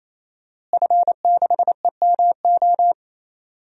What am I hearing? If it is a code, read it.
Morse code, heard as a keyed tone, about 28 wpm, 700 Hz. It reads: F6EMO
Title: DXCC-100 Call Signs - 5 Characters - Repeated 3x - 28wpm